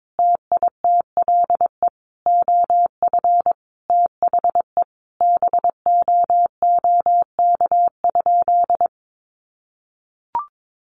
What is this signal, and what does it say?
Morse code, heard as a keyed tone, about 22 wpm, 700 Hz. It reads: TITLE OF THE BOOK?
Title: Sentences from Top 1200 Words 22wpm